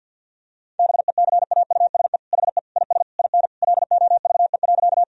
Morse code, heard as a keyed tone, about 51 wpm, 700 Hz. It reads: BECAUSE HE IS IN LOVE.
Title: Sentences from Top 1000 Words - Review Part 3 - 50wpm